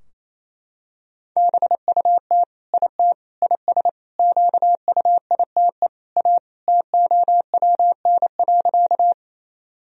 Morse code, heard as a keyed tone, about 28 wpm, 700 Hz. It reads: BUT IT IS QUITE A TOWN.
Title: Sentences from Top 1000 Words - Review Part 2 - 28wpm